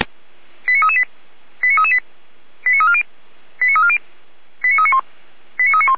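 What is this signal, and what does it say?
Five-tone selective call (ZVEI), heard as quick runs of five tones, tones of about 70 ms. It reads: 89338 89338 89340 89340 89391 89391
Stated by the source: Received on an SDR, ZVEI five-tone code 89280.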